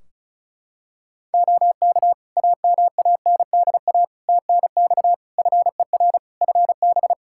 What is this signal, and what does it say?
Morse code, heard as a keyed tone, about 34 wpm, 700 Hz. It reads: OK AMANDA TNX FER FB
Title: Practice QSO - Exchange 4 - Ending - 35wpm